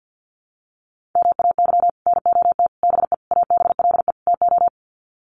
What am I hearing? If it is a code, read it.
Morse code, heard as a keyed tone, about 50 wpm, 700 Hz. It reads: MAY NOT BE ABLE TO
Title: Sets of 5 Words 50wpm